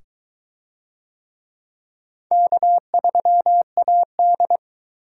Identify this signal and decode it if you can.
Morse code, heard as a keyed tone, about 22 wpm, 700 Hz. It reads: K3AD